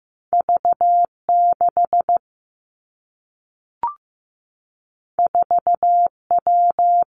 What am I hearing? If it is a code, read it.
Morse code, heard as a keyed tone, about 15 wpm, 700 Hz. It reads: V6 4W
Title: Two Challenging Letter-Number Combinations 15wpm